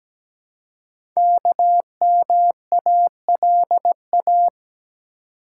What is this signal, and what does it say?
Morse code, heard as a keyed tone, about 17 wpm, 700 Hz. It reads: KMALA
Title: Code Groups - 5 Letters - 17wpm